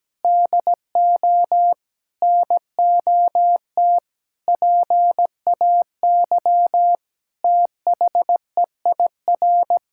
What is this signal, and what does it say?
Morse code, heard as a keyed tone, about 17 wpm, 700 Hz. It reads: DO NOT PAY THEIR
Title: Sentences from Top 1000 Words - Review Part 5 - 17wpm